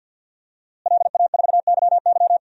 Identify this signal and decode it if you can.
Morse code, heard as a keyed tone, about 52 wpm, 700 Hz. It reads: LA4YY
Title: DXCC-100 Call Signs - 5 Characters - Repeated 2x - 50wpm